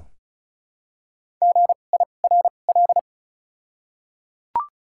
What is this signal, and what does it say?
Morse code, heard as a keyed tone, about 36 wpm, 700 Hz. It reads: G I R L
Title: 2x Character Spacing - Top 500 Words - 35wpm